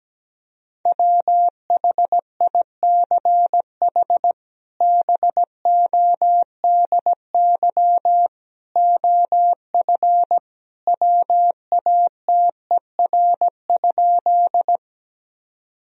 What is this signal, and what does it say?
Morse code, heard as a keyed tone, about 17 wpm, 700 Hz. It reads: WHICH BODY OF WATER?
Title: Sentences from Top 1000 Words - Review Part 2 - 17wpm